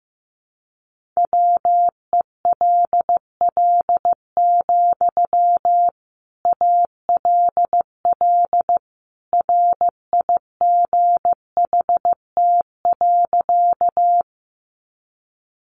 Morse code, heard as a keyed tone, about 15 wpm, 700 Hz. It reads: WELL, ALL RIGHT.
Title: Sentences from Top 200 Words 15wpm